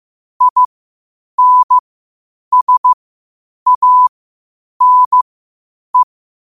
Morse code, heard as a keyed tone, about 15 wpm, spaced as for 8 wpm, 1000 Hz. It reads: INSANE
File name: qotdfiles / 8wpm